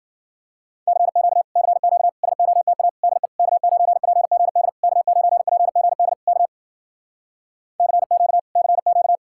Morse code, heard as a keyed tone, about 52 wpm, 700 Hz. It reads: CQ CQ SOTA DE K9PGD K9PGD K CQ CQ